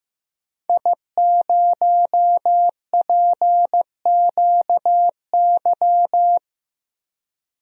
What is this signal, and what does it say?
Morse code, heard as a keyed tone, about 15 wpm, 700 Hz. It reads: I0PQY
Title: DXCC-100 Call Signs - 5 Characters - 15wpm